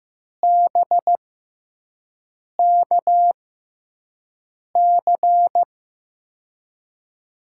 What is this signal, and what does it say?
Morse code, heard as a keyed tone, about 15 wpm, spaced as for 5 wpm, 700 Hz. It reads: BKC
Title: USA Call Signs - 6x Character Spacing - 15wpm